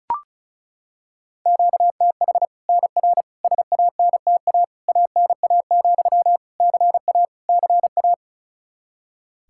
Morse code, heard as a keyed tone, about 36 wpm, 700 Hz. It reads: QTH NR SANTA ANA, CA CA
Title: Practice QSO - Exchange 1 - Beginning - 35wpm